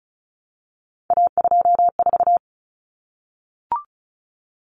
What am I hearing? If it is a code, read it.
Morse code, heard as a keyed tone, about 34 wpm, 700 Hz. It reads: A24
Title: Common Cut Numbers - 3-Digit 35wpm